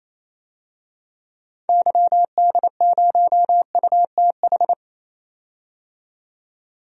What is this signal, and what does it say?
Morse code, heard as a keyed tone, about 28 wpm, 700 Hz. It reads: YD0UTH
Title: DXCC-100 Call Signs - 6 Characters - 28wpm